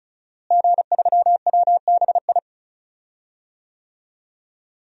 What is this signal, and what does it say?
Morse code, heard as a keyed tone, about 36 wpm, 700 Hz. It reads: G3WBI
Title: DXCC-100 Call Signs - 5 Characters - Repeated 3x - 35wpm